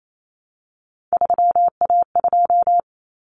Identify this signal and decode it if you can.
Morse code, heard as a keyed tone, about 28 wpm, 700 Hz. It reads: 3A2
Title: Common Cut Numbers - 3-Digit 28wpm